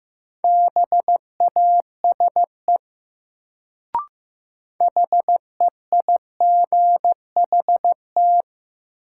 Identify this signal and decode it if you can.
Morse code, heard as a keyed tone, about 15 wpm, 700 Hz. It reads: BASE HEIGHT
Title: Sets of 5 Words - New Word Drill - 15wpm